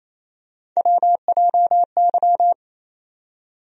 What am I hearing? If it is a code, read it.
Morse code, heard as a keyed tone, about 28 wpm, 700 Hz. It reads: WJY
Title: USA Call Sign Postfixes - 3 Letters 28wpm